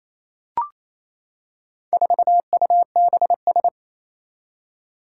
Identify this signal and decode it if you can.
Morse code, heard as a keyed tone, about 27 wpm, 700 Hz. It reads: 4UBS
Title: Four Challenging Letter-Number Combinations 28wpm